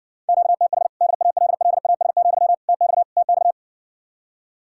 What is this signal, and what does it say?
Morse code, heard as a keyed tone, about 60 wpm, 700 Hz. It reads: QTH DALLAS, TX TX